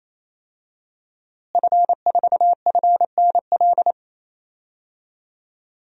Morse code, heard as a keyed tone, about 27 wpm, 700 Hz. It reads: F4FNL
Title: DXCC-100 Call Signs - 5 Characters - Repeated 3x - 28wpm